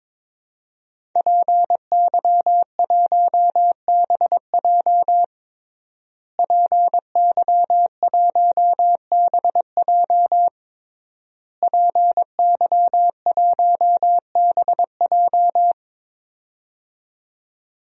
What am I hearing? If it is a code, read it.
Morse code, heard as a keyed tone, about 22 wpm, 700 Hz. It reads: PY1BJ PY1BJ PY1BJ
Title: DXCC-100 Call Signs - 5 Characters - Repeated 3x - 22wpm